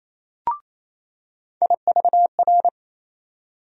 Morse code, heard as a keyed tone, about 28 wpm, 700 Hz. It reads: IVR
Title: USA Call Sign Postfixes - 3 Letters 28wpm